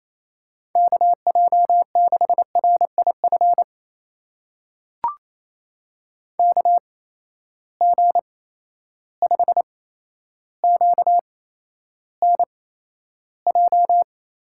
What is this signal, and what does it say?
Morse code, heard as a keyed tone, about 28 wpm, spaced as for 12 wpm, 700 Hz. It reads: KJ6RIF KG5QNJ
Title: USA Call Signs - 8x Character Spacing - 28wpm